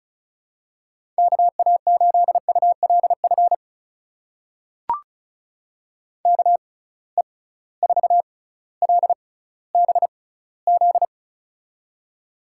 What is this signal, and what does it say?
Morse code, heard as a keyed tone, about 36 wpm, spaced as for 18 wpm, 700 Hz. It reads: KA8ULF KE4LBZ
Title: USA Call Signs - 6 Characters - 6x Character Spacing - 35wpm